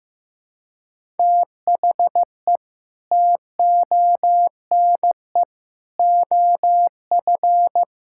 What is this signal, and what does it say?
Morse code, heard as a keyed tone, about 14 wpm, 700 Hz. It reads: THE TONE OF
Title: Sentences from Top 1600 Words 15wpm